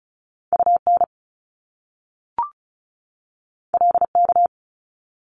Morse code, heard as a keyed tone, about 36 wpm, 700 Hz. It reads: UN LK
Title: Top Two Letter Combinations in Words 35wpm